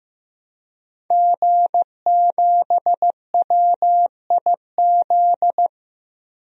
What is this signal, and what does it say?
Morse code, heard as a keyed tone, about 15 wpm, 700 Hz. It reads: G7WIZ
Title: DXCC-100 Call Signs - 5 Characters - Repeated 3x - 15wpm